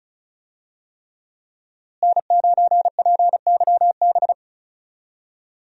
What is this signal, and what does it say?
Morse code, heard as a keyed tone, about 34 wpm, 700 Hz. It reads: N9PYB
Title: USA Call Signs - 5 Characters - Repeated 3x - 35wpm